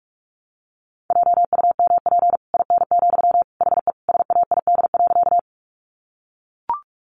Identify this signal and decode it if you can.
Morse code, heard as a keyed tone, about 44 wpm, 700 Hz. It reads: JUMP IN, HE SAID.
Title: Sentences from Top 1300 Words 45wpm